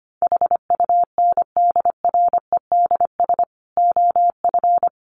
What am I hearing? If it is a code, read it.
Morse code, heard as a keyed tone, about 25 wpm, 700 Hz. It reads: HUNDREDS OF
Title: Sets of 5 Words Encore2 25wpm